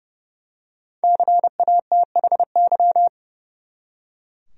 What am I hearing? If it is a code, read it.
Morse code, heard as a keyed tone, about 30 wpm, 700 Hz. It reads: CATHY
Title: USA Common Names - 30wpm